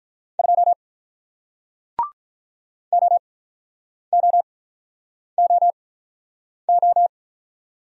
Morse code, heard as a keyed tone, about 48 wpm, spaced as for 9 wpm, 700 Hz. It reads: 2 OOOO